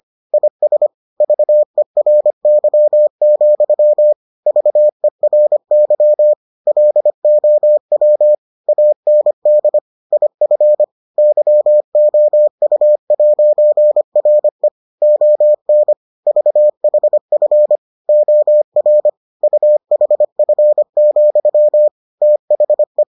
Morse code, heard as a keyed tone, about 25 wpm, 600 Hz. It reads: IS VERY, VERY LOW AND IF YOU'RE ON VHF OR UHF, THE